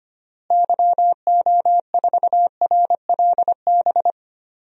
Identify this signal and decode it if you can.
Morse code, heard as a keyed tone, about 25 wpm, 700 Hz. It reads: YO4RLB
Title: DXCC-100 Call Signs - 6 Characters - Repeated 2x - 25wpm